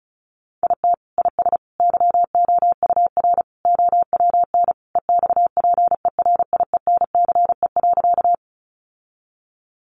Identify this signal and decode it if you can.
Morse code, heard as a keyed tone, about 34 wpm, 700 Hz. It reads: IT IS YOUR OWN EXPERIENCE.
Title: Sentences from Top 600 Words 35wpm